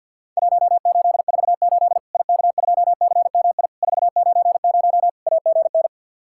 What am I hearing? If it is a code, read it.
Morse code, heard as a keyed tone, about 50 wpm, 700 Hz. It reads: 1848 IK2CMI 490 AGN
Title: Worked All Europe DX Contest - QTC Exchange with Repeats - 50wpm